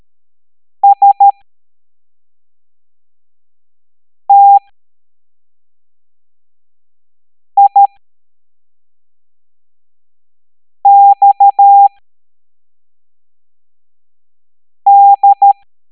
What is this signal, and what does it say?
Morse code, heard as a keyed tone, about 13 wpm, spaced as for 3 wpm, 800 Hz. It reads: STIXD
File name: hamdata / L05-EX01-SVY